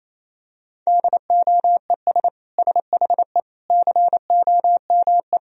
Morse code, heard as a keyed tone, about 27 wpm, 700 Hz. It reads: DOES SHE COME
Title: Sentences from Top 1000 Words - Review Part 4 - 28wpm